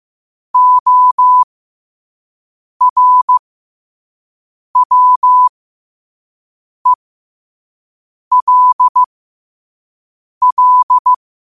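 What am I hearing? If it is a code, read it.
Morse code, heard as a keyed tone, about 15 wpm, spaced as for 5 wpm, 1000 Hz. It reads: ORWELL
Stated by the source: Morse code sent at 5 wpm.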